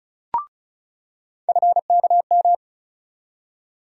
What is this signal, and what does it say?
Morse code, heard as a keyed tone, about 36 wpm, 700 Hz. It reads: FKM
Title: USA Call Sign Postfixes - 3 Letters 35wpm